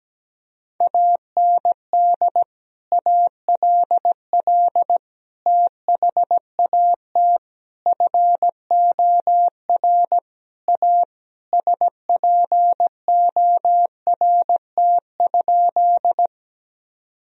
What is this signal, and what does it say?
Morse code, heard as a keyed tone, about 17 wpm, 700 Hz. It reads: AND ALL THAT FOR A SPORT?
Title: Sentences from Top 1000 Words - Review Part 4 - 17wpm